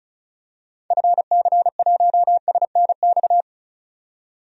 Morse code, heard as a keyed tone, about 34 wpm, 700 Hz. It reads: FC1SNX
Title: DXCC-100 Call Signs - 6 Characters - 35wpm